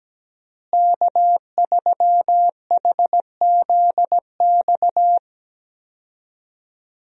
Morse code, heard as a keyed tone, about 17 wpm, 700 Hz. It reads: K3HZX
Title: USA Call Signs - 5 Characters - Repeated 2x - 17wpm